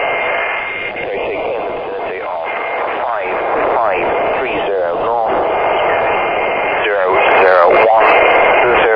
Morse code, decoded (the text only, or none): none